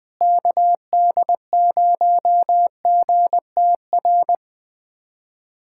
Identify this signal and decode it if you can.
Morse code, heard as a keyed tone, about 20 wpm, 700 Hz. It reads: KD0GTR
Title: USA Callsign DE Callsign - 6 Characters - 20wpm